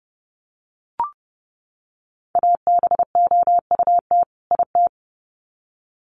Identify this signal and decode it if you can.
Morse code, heard as a keyed tone, about 30 wpm, 700 Hz. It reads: ABOUT IT
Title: Sets of 2 Words 30wpm